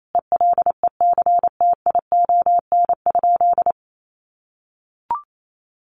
Morse code, heard as a keyed tone, about 28 wpm, 700 Hz. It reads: ELECTION?
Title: Sentences from Top 1500 Words 28wpm